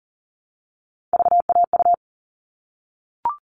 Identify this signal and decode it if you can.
Morse code, heard as a keyed tone, about 40 wpm, 700 Hz. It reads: VAU
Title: All Cut Numbers - 3-Digit 40wpm